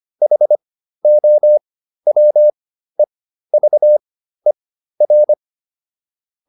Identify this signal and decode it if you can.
Morse code, heard as a keyed tone, about 25 wpm, spaced as for 13 wpm, 600 Hz. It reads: HOWEVER